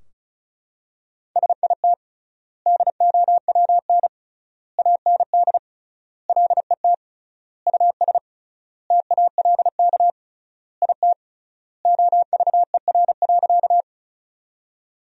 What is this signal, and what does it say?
Morse code, heard as a keyed tone, about 36 wpm, 700 Hz. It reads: SIT DOWN AND LET US TALK IT OVER.
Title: 3x Word-Spacing - Sentences from Top 500 Words - 35wpm